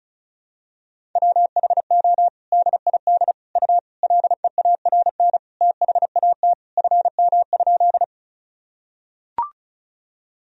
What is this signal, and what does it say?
Morse code, heard as a keyed tone, about 34 wpm, 700 Hz. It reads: WHO DID U LEARN THAT FM?
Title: Sentences from Top 300 Words and Abbreviations 35wpm